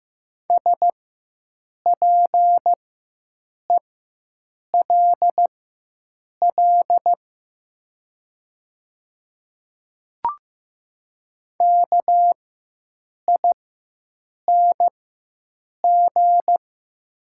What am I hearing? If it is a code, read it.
Morse code, heard as a keyed tone, about 15 wpm, spaced as for 7 wpm, 700 Hz. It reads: SPELL KING
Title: 4x Character Spacing - Top 500 Words - 15wpm